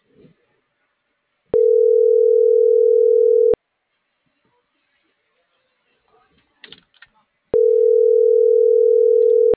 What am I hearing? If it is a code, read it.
Ringback tone (North America), 440 + 480 Hz.